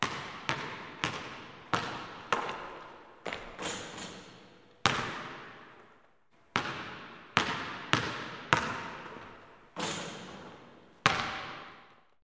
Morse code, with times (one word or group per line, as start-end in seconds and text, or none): none